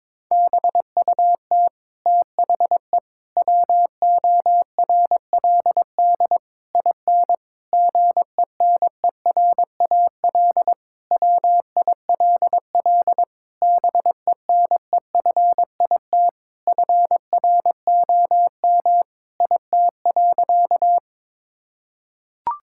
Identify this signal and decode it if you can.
Morse code, heard as a keyed tone, about 22 wpm, 700 Hz. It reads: BUT THE WORLD IN GENERAL WILL BENEFIT FROM IT.
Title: Sentences from Top 1000 Words - Review Part 6 - 22wpm